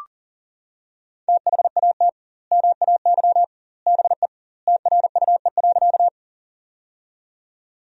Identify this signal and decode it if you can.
Morse code, heard as a keyed tone, about 40 wpm, 700 Hz. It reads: THAT MAY BE TRUE.
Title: 2x Word-Spacing - Sentences from Top 400 Words - 40wpm